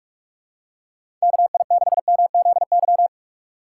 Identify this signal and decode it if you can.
Morse code, heard as a keyed tone, about 46 wpm, 700 Hz. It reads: KI6MZY
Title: USA Call Signs - 6 Characters - 4x Character Spacing - 45wpm